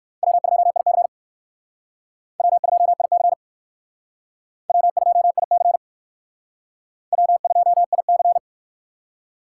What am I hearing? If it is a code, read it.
Morse code, heard as a keyed tone, about 65 wpm, 700 Hz. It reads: W2IC W2IC W2IC W2IC